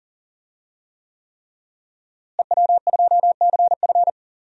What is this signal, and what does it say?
Morse code, heard as a keyed tone, about 40 wpm, 700 Hz. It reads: EW2CF
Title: DXCC-100 Call Signs - 5 Characters - Repeated 3x - 40wpm